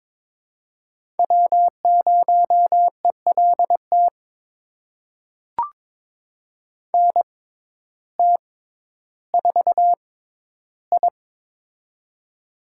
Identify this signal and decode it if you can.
Morse code, heard as a keyed tone, about 22 wpm, spaced as for 12 wpm, 700 Hz. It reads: W0ELT NT4I
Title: USA Call Signs - 6x Character Spacing - 22wpm